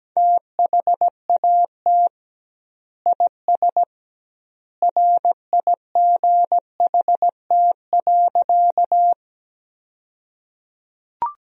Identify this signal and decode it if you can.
Morse code, heard as a keyed tone, about 17 wpm, 700 Hz. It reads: THAT IS RIGHT.
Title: 2x Word-Spacing - Sentences from Top 200 Words - 17wpm